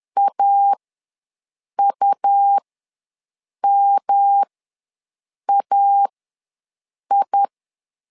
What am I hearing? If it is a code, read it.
Morse code, heard as a keyed tone, about 10 wpm, spaced as for 6 wpm, 800 Hz. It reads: AUMAI
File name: LEZIONI / 3 MWRO / 3 M ws 10wpm 2121